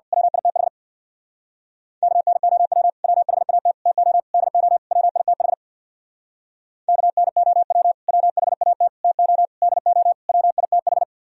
Morse code, heard as a keyed tone, about 55 wpm, 700 Hz. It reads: WITH KNOW WHAT TO DO WITH KNOW WHAT TO DO WITH
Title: Speed-Racing - Sets of 5 Words 50wpm